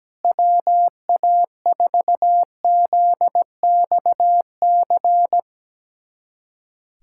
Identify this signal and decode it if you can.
Morse code, heard as a keyed tone, about 17 wpm, 700 Hz. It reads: WA4ZXC
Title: USA Callsign DE Callsign 17wpm